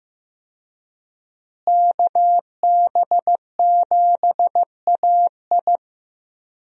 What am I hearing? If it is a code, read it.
Morse code, heard as a keyed tone, about 15 wpm, 700 Hz. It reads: KB7AI
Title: USA Call Signs - 5 Characters - Repeated 3x - 15wpm